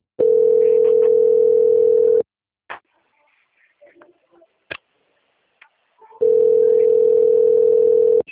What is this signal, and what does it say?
Ringback tone (North America), 440 + 480 Hz.
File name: out- - - - - .59494